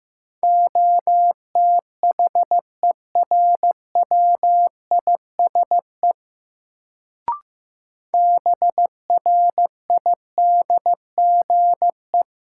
Morse code, heard as a keyed tone, about 15 wpm, 700 Hz. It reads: OTHERWISE BRIDGE
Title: Sentences from Top 1200 Words - New Word Drill - 15wpm